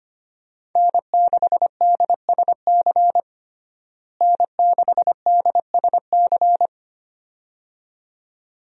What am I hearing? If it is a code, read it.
Morse code, heard as a keyed tone, about 25 wpm, 700 Hz. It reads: N6DSC N6DSC
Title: USA Call Signs - 5 Characters - Repeated 2x - 25wpm